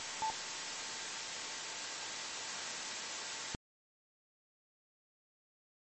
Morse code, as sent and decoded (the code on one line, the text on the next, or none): none